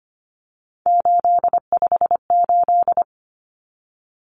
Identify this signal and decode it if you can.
Morse code, heard as a keyed tone, about 25 wpm, 700 Hz. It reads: 858
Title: Numbers - 3-Digit 25wpm